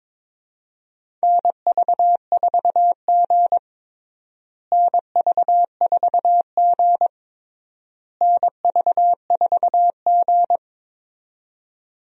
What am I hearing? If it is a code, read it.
Morse code, heard as a keyed tone, about 21 wpm, 700 Hz. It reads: NV4G NV4G NV4G
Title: USA Call Signs - 4 Characters - Repeated 3x - 22wpm